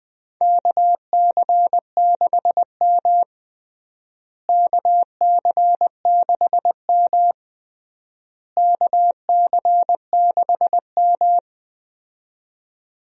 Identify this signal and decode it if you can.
Morse code, heard as a keyed tone, about 20 wpm, 690 Hz. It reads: KC6M KC6M KC6M
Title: USA Call Signs - 4 Characters - Repeated 3x - 20wpm